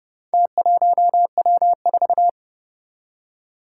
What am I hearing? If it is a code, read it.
Morse code, heard as a keyed tone, about 30 wpm, 700 Hz. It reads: T1W4
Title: Four Challenging Letter-Number Combinations 30wpm